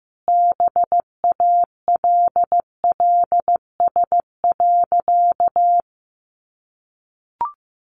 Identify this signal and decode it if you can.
Morse code, heard as a keyed tone, about 15 wpm, 700 Hz. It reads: BALLS.
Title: Easy Phrases 15wpm